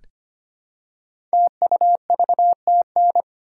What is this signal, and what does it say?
Morse code, heard as a keyed tone, about 25 wpm, 700 Hz. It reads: TUVTN
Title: Code Groups - 5 Letters - 25wpm